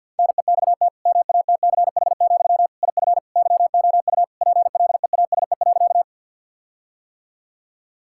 Morse code, heard as a keyed tone, about 50 wpm, 700 Hz. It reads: NEXT MATCH, IF YOU PLEASE.
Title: Sentences from Top 1600 Words 50wpm